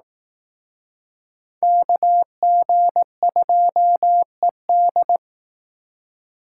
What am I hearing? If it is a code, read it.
Morse code, heard as a keyed tone, about 19 wpm, 700 Hz. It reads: KG2ED